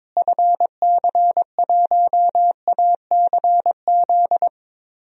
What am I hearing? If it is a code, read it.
Morse code, heard as a keyed tone, about 22 wpm, 700 Hz. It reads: FC1ACZ
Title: DXCC-100 Call Signs - 6 Characters - Repeated 2x - 22wpm